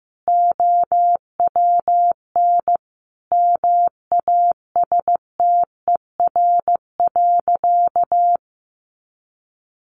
Morse code, heard as a keyed tone, about 15 wpm, 700 Hz. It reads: OWN MASTER.
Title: Sentences from Top 1300 Words 15wpm